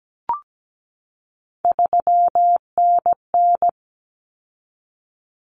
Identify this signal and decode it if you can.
Morse code, heard as a keyed tone, about 17 wpm, 700 Hz. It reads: 3NN
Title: Common Cut Numbers - 3-Digit 17wpm